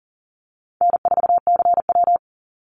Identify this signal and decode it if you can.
Morse code, heard as a keyed tone, about 40 wpm, 700 Hz. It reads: N4CW
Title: Call Signs - 40wpm